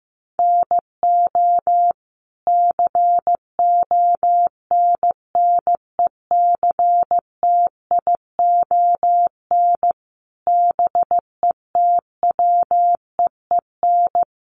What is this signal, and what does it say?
Morse code, heard as a keyed tone, about 14 wpm, 700 Hz. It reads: NO CONNECTION BETWEEN
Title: Sentences from Top 1300 Words 15wpm